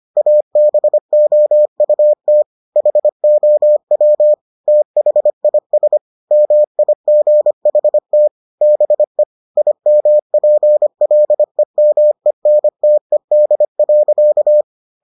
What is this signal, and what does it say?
Morse code, heard as a keyed tone, about 25 wpm, 600 Hz. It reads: ABOUT HOW THIS MIGHT BE IMPLEMENTED.